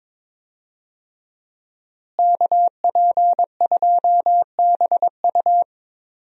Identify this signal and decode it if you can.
Morse code, heard as a keyed tone, about 22 wpm, 700 Hz. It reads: KP2BU
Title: USA Call Signs - 5 Characters - Repeated 3x - 22wpm